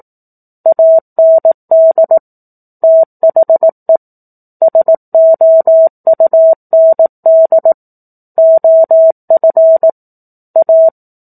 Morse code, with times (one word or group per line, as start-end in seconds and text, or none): 0.7-2.2 AND
2.8-4.0 THE
4.6-7.7 SOUND
8.4-9.9 OF
10.6-10.9 A